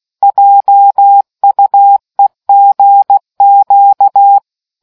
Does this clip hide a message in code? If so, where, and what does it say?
Yes: Morse code from 0.2 to 4.4 s, heard as a keyed tone, about 16 wpm, 800 Hz. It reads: JUEGQ